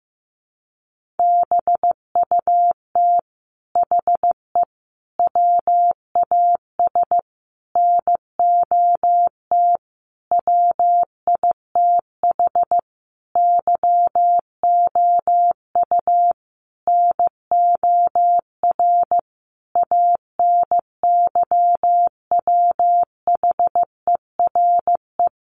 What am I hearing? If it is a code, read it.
Morse code, heard as a keyed tone, about 15 wpm, 700 Hz. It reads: BUT HE WAS NOT WITH YOU NOR ANYWHERE
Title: Sentences from Top 1500 Words 15wpm